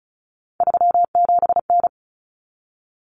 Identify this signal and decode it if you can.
Morse code, heard as a keyed tone, about 34 wpm, 700 Hz. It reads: 37N
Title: Common Cut Numbers - 3-Digit 35wpm